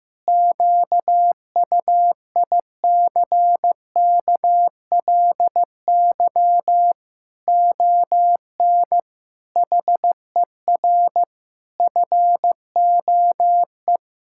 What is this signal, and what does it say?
Morse code, heard as a keyed tone, about 15 wpm, 700 Hz. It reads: QUICKLY ON HER FOE
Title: Pangram Challenge v2 - 15wpm